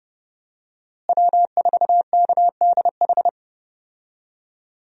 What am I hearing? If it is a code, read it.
Morse code, heard as a keyed tone, about 30 wpm, 700 Hz. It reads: W4KDH
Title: USA Call Signs - 5 Characters - Repeated 2x - 30wpm